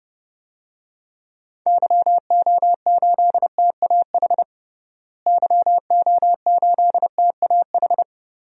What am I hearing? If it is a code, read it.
Morse code, heard as a keyed tone, about 30 wpm, 700 Hz. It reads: YO8TAH YO8TAH
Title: DXCC-100 Call Signs - 6 Characters - Repeated 2x - 30wpm